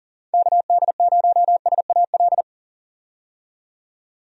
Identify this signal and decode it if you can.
Morse code, heard as a keyed tone, about 40 wpm, 700 Hz. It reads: KD0SAL
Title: USA Call Signs - 6 Characters - 40wpm